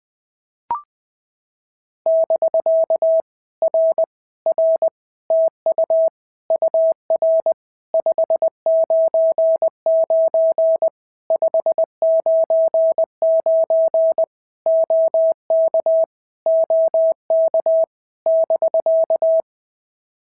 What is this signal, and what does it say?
Morse code, heard as a keyed tone, about 20 wpm, 650 Hz. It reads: <BK> R R TU UR 599 599 OK OK <BK>
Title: POTA QSO Practice - 20wpm